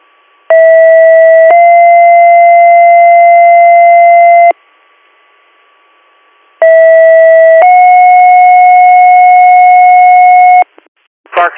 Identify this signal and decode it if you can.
Two-tone page, heard as a one-second tone, then a three-second tone, about 650 then 690 Hz, about 650 then 730 Hz, with speech. It is heard from 0.5 to 10.6 s.